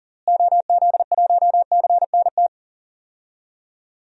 Morse code, heard as a keyed tone, about 40 wpm, 700 Hz. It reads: OZ1CNT
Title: DXCC-100 Call Signs - 6 Characters - Repeated 2x - 40wpm